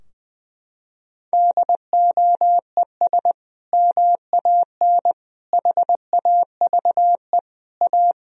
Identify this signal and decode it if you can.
Morse code, heard as a keyed tone, about 20 wpm, 700 Hz. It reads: DOES MAN HAVE A
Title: Sentences from Top 1000 Words - Review Part 3 - 20wpm